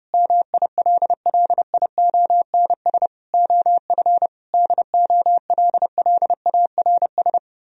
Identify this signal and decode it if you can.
Morse code, heard as a keyed tone, about 30 wpm, 700 Hz. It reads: MILLIONS OF DOLLARS